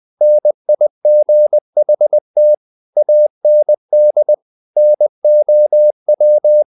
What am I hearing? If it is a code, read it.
Morse code, heard as a keyed tone, about 20 wpm, 600 Hz. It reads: NIGHT AND NOW
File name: the old man and the sea part 2 20wpm